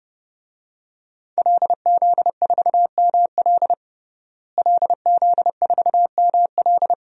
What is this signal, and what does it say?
Morse code, heard as a keyed tone, about 32 wpm, 700 Hz. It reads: LZ4ML LZ4ML
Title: DXCC-100 Call Signs - 5 Characters - Repeated 2x - 30wpm